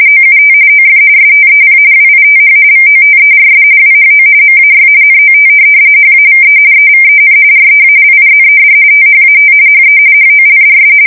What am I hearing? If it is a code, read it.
RTTY, about 45.45 baud. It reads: OUT OF STEP WITH VOTERS WHO MAY NOT HAVE EVEN REALIZED THAT HE HOLDS
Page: 170 Shift 2125/2295